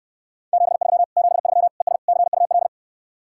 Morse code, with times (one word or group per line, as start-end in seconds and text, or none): 0.5-1.0 73
1.2-1.7 73
1.8-2.0 ES
2.1-2.7 CUL